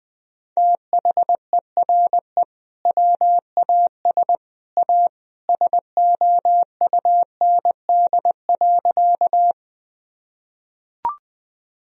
Morse code, heard as a keyed tone, about 20 wpm, 700 Hz. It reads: THERE WAS A SOUND.
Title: Sentences from Top 1000 Words - Review Part 1 - 20wpm